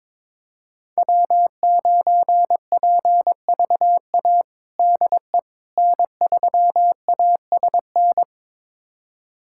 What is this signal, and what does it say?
Morse code, heard as a keyed tone, about 22 wpm, 700 Hz. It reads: W9PVA DE N3ASN
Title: USA Callsign DE Callsign - 5 Characters - 22wpm